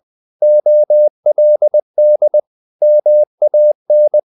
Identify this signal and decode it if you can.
Morse code, heard as a keyed tone, about 20 wpm, 600 Hz. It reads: OLD MAN
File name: the old man and the sea / the old man and the sea part 1 20wpm